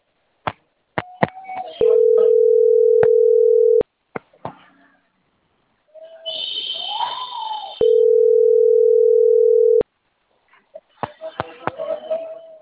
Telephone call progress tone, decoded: ringback tone